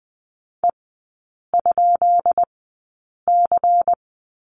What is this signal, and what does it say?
Morse code, heard as a keyed tone, about 20 wpm, spaced as for 8 wpm, 700 Hz. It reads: E?C
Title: 166 Lesson 27e - Number 7 - Three Characters 20wpm